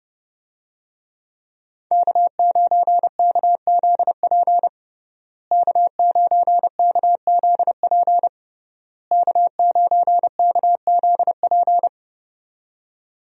Morse code, heard as a keyed tone, about 30 wpm, 700 Hz. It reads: K9KZP K9KZP K9KZP
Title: USA Call Signs - 5 Characters - Repeated 3x - 30wpm